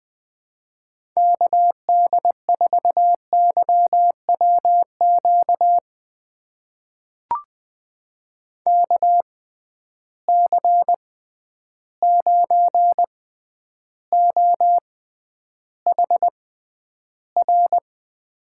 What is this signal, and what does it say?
Morse code, heard as a keyed tone, about 21 wpm, spaced as for 10 wpm, 700 Hz. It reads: KD4YWQ KC9OHR
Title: USA Call Signs - 6 Characters - 6x Character Spacing - 20wpm